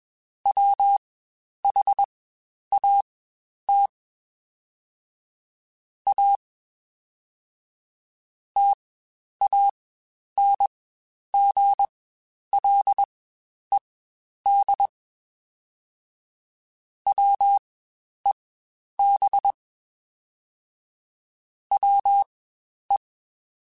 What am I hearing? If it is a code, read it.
Morse code, heard as a keyed tone, about 21 wpm, spaced as for 10 wpm, 800 Hz. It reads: WHAT A TANGLED WEB WE
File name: Quotes Spaced / Quotes Spaced-mp3s / Deceive sp @20